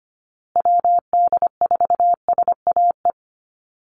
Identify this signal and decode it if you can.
Morse code, heard as a keyed tone, about 25 wpm, 700 Hz. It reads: WD4SAE
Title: USA Call Signs Repeated 3x 25wpm